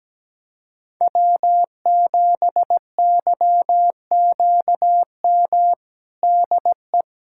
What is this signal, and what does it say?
Morse code, heard as a keyed tone, about 17 wpm, 700 Hz. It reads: W7YQM DE
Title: USA Callsign DE Callsign - 5 Characters - 17wpm